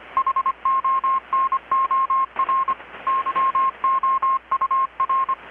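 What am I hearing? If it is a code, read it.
Morse code, heard as a keyed tone, about 25 wpm, 1070 Hz, with radio noise. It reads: HONOR YOUR
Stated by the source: Morse from KPH's Night of Nights XX maritime radiotelegraphy event, received in Chiapas, Mexico about 3,800 km away on a Kenwood R-600 with a grounded 10 m long-wire antenna.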